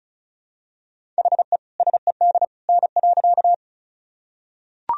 Morse code, heard as a keyed tone, about 36 wpm, 700 Hz. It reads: HE SED N.
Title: Sentences from Top 100 Words and Abbreviations 35wpm